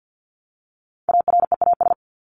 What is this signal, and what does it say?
Morse code, heard as a keyed tone, about 48 wpm, 700 Hz. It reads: AREAS
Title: ES:S Suffix - Top 100 Words with Base Words 50wpm